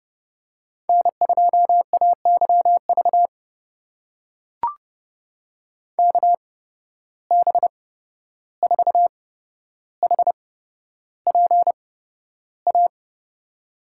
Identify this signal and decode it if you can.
Morse code, heard as a keyed tone, about 30 wpm, spaced as for 12 wpm, 700 Hz. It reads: N2AYV KB4HPA
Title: USA Call Signs - 8x Character Spacing - 30wpm